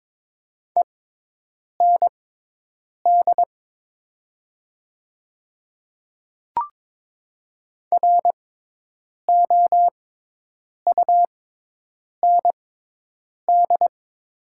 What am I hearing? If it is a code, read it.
Morse code, heard as a keyed tone, about 22 wpm, spaced as for 8 wpm, 700 Hz. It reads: END ROUND
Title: 6x Character Spacing - Top 200 Words - 22wpm